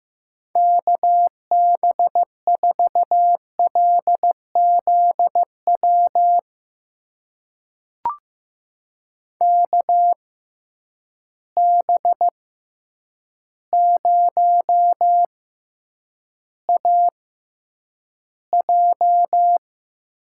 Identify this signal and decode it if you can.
Morse code, heard as a keyed tone, about 15 wpm, spaced as for 8 wpm, 700 Hz. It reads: KB4LZW KB0AJ
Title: USA Call Signs - 6 Characters - 6x Character Spacing - 15wpm